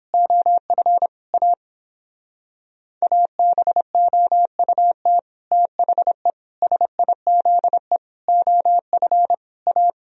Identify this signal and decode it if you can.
Morse code, heard as a keyed tone, about 29 wpm, 700 Hz. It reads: OF A ABOUT THE SIZE OF A